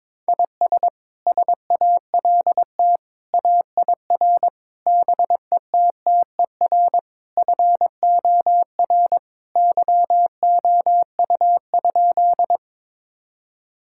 Morse code, heard as a keyed tone, about 22 wpm, 700 Hz. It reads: IS SALT AIR BETTER FOR YOU?
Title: Sentences from Top 1000 Words - Review Part 5 - 22wpm